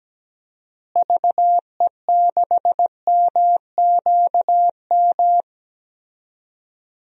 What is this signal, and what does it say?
Morse code, heard as a keyed tone, about 17 wpm, 700 Hz. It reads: VE6MQM
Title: DXCC-100 Call Signs - 6 Characters - Repeated 2x - 17wpm